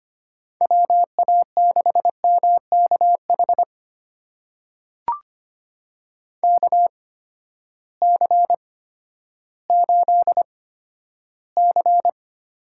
Morse code, heard as a keyed tone, about 25 wpm, spaced as for 12 wpm, 700 Hz. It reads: WA6MKH KC8C
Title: USA Call Signs - 6 Characters - 8x Character Spacing - 25wpm